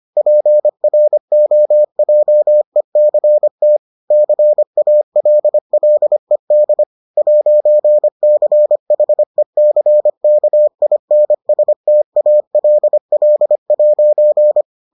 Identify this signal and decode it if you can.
Morse code, heard as a keyed tone, about 25 wpm, 600 Hz. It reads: PROJECT CALLED 'CHECKINSTALL'